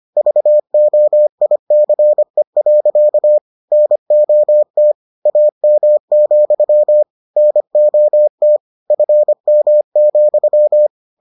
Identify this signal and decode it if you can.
Morse code, heard as a keyed tone, about 25 wpm, 600 Hz. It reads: VOICE. NOT AM, NOT FM,